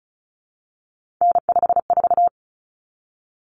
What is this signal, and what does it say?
Morse code, heard as a keyed tone, about 34 wpm, 700 Hz. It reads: N54
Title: Common Cut Numbers - 3-Digit 35wpm